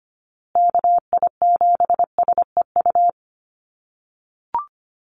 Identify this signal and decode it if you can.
Morse code, heard as a keyed tone, about 25 wpm, 700 Hz. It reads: KI7SEU
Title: USA Call Signs Repeated 3x 25wpm